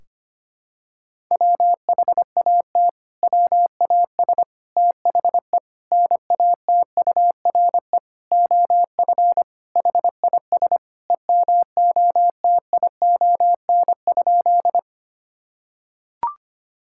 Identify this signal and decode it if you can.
Morse code, heard as a keyed tone, about 25 wpm, 700 Hz. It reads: WHAT WAS THE NATURE OF HIS EMOTION?
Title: Sentences from Top 1600 Words 25wpm